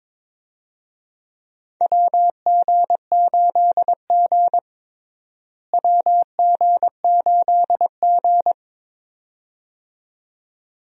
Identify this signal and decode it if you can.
Morse code, heard as a keyed tone, about 22 wpm, 700 Hz. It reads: WG8G WG8G
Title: USA Call Signs - 4 Characters - Repeated 2x - 22wpm